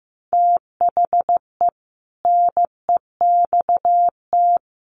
Morse code, heard as a keyed tone, about 15 wpm, 700 Hz. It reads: THE NEXT
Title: Sentences from Top 600 Words 15wpm